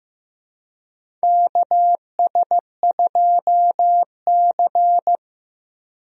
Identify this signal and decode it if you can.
Morse code, heard as a keyed tone, about 15 wpm, 700 Hz. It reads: KS2C
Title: USA Call Signs - 4 Characters - Repeated 2x - 15wpm